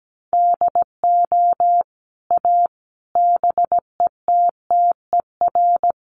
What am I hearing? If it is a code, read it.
Morse code, heard as a keyed tone, about 17 wpm, 700 Hz. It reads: DO A BETTER
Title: Sets of 5 Words Encore2 17wpm